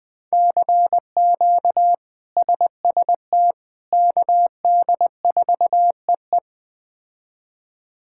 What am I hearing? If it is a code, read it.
Morse code, heard as a keyed tone, about 20 wpm, 700 Hz. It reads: CQ SST KD4EE
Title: K1USN SST QSO Practice - 20wpm